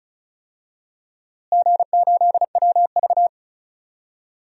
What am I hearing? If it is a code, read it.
Morse code, heard as a keyed tone, about 36 wpm, 690 Hz. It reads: G8WV
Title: DXCC-100 Call Signs - 4 Characters - Repeated 3x - 35wpm